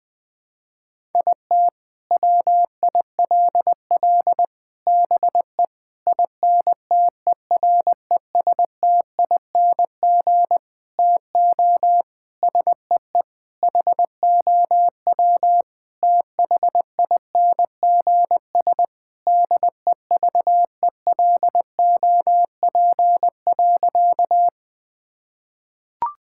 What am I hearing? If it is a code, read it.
Morse code, heard as a keyed tone, about 20 wpm, 700 Hz. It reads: IT WILL BE INTERESTING TO SEE HOW THINGS DEVELOP.
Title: Sentences with 6-Word Phrases - Spoken Beginning - 20wpm